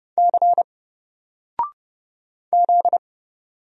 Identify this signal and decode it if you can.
Morse code, heard as a keyed tone, about 30 wpm, 700 Hz. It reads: C Z
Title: Single Character (Letter:Number:Punctuation:Prosign) 30wpm